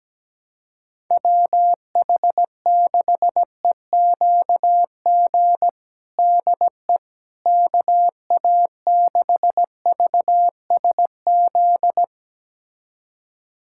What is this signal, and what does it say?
Morse code, heard as a keyed tone, about 17 wpm, 700 Hz. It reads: WH6EQG DE KA6VSZ
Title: USA Callsign DE Callsign - 6 Characters - 17wpm